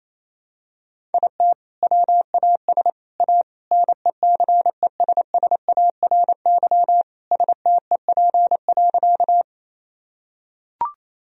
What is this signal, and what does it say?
Morse code, heard as a keyed tone, about 28 wpm, 700 Hz. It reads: IT WAS A NECESSARY STEP.
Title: Sentences from Top 1000 Words - Review Part 2 - 28wpm